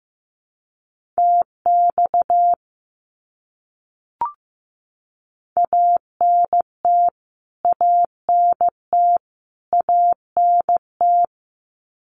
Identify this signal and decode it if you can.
Morse code, heard as a keyed tone, about 15 wpm, 700 Hz. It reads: TX ANT ANT ANT
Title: Abbreviations Repeated 3x - 15wpm